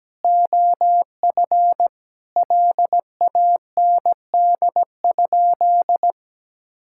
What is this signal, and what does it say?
Morse code, heard as a keyed tone, about 17 wpm, 700 Hz. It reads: OF LAND?
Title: Sentences from Top 1600 Words 17wpm